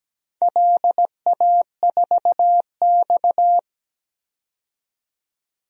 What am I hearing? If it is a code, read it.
Morse code, heard as a keyed tone, about 17 wpm, 700 Hz. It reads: LA4X
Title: DXCC-100 Call Signs - 4 Characters - 17wpm